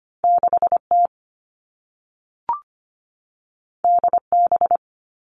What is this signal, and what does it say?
Morse code, heard as a keyed tone, about 25 wpm, 700 Hz. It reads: 6T DB
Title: All Cut Numbers - 2-Digit 25wpm